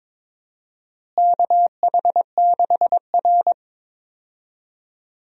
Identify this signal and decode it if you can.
Morse code, heard as a keyed tone, about 21 wpm, 700 Hz. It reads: KH6R
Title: DXCC-100 Call Signs - 4 Characters - Repeated 2x - 22wpm